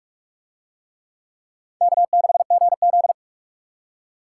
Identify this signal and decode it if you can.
Morse code, heard as a keyed tone, about 42 wpm, 700 Hz. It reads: K6GZ